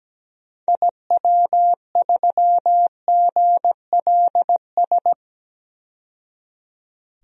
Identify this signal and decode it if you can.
Morse code, heard as a keyed tone, about 17 wpm, 700 Hz. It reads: IW3GLS
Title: DXCC-100 Call Signs - 6 Characters - 17wpm